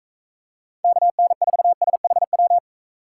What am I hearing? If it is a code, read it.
Morse code, heard as a keyed tone, about 35 wpm, 700 Hz. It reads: KN4SHW